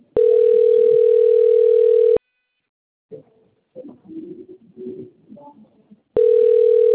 Ringback tone (North America), 440 + 480 Hz.